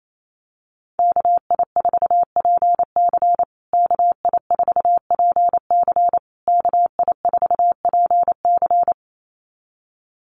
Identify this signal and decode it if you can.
Morse code, heard as a keyed tone, about 28 wpm, 700 Hz. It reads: KI4PC KI4PC KI4PC
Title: USA Call Signs Repeated 3x 28wpm